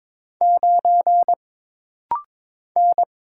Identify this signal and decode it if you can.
Morse code, heard as a keyed tone, about 22 wpm, 700 Hz. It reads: 9 N
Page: This rapid-fire timing is geared to push your ICR (Instant Character Recognition) skills.